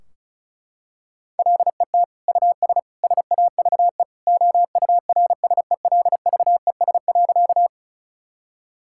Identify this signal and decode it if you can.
Morse code, heard as a keyed tone, about 34 wpm, 700 Hz. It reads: LET US SAVE OURSELVES.
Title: Sentences from Top 1600 Words 35wpm